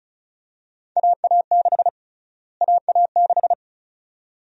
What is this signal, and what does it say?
Morse code, heard as a keyed tone, about 36 wpm, 700 Hz. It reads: AA6 AA6
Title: USA Call Sign Prefixes - Repeated 2x - 35wpm